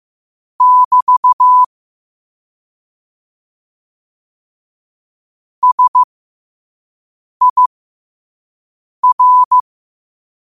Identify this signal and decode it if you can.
Morse code, heard as a keyed tone, about 15 wpm, spaced as for 5 wpm, 1000 Hz. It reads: <BT> SIR
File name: qotdfiles / 5wpm